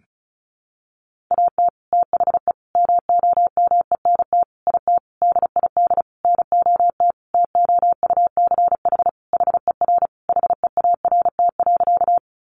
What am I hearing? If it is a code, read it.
Morse code, heard as a keyed tone, about 34 wpm, 700 Hz. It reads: AT THE MOMENT IT DID NOT TOUCH HER HEART.
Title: Sentences from Top 1200 Words 35wpm